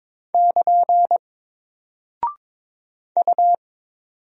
Morse code, heard as a keyed tone, about 22 wpm, 700 Hz. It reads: <KN> U
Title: Single Character (Letter:Number:Punctuation:Prosign) Alt 22wpm